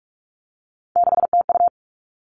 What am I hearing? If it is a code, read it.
Morse code, heard as a keyed tone, about 44 wpm, 700 Hz. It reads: 6TU
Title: All Cut Numbers - 3-Digit 45wpm